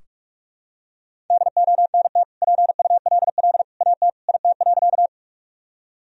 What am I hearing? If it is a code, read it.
Morse code, heard as a keyed tone, about 46 wpm, 700 Hz. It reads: DONT PULL AT IT.
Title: Sentences from Top 1000 Words - Review Part 2 - 45wpm